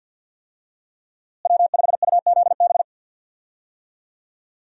Morse code, heard as a keyed tone, about 52 wpm, 700 Hz. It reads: W5UZB
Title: USA Call Signs - 5 Characters - Repeated 3x - 50wpm